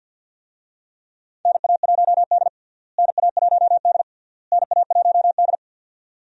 Morse code, heard as a keyed tone, about 47 wpm, 700 Hz. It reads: NA1D NA1D NA1D
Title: USA Call Signs - 4 Characters - Repeated 3x - 50wpm